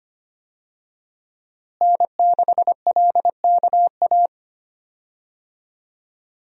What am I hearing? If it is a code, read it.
Morse code, heard as a keyed tone, about 25 wpm, 700 Hz. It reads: N6LKA
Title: USA Call Signs - 5 Characters - Repeated 3x - 25wpm